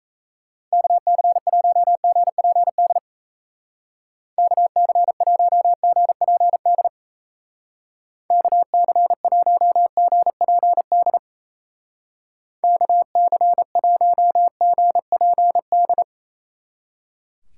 Morse code, heard as a keyed tone, about 35 wpm, 700 Hz. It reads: KC1GPD KC1GPD KC1GPD KC1GPD
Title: Speed Racing - USA 2x3 Call Signs 28wpm